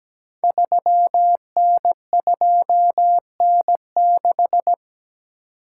Morse code, heard as a keyed tone, about 17 wpm, 690 Hz. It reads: 3N2N6
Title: Code Groups - 5 Letter:Numbers - 17wpm